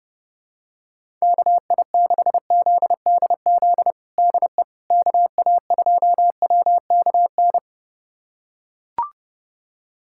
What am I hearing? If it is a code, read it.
Morse code, heard as a keyed tone, about 32 wpm, 700 Hz. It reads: KI6ZDZ DE KA2WKN
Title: USA Callsign DE Callsign 30wpm